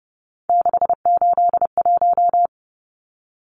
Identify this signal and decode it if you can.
Morse code, heard as a keyed tone, about 30 wpm, 700 Hz. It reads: 681
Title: Numbers - 3-Digit 30wpm